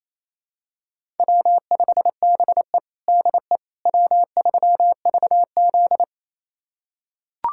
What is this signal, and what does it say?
Morse code, heard as a keyed tone, about 27 wpm, 700 Hz. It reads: W5BE DE W3VZ
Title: USA Callsign DE Callsign - 4 Characters - 28wpm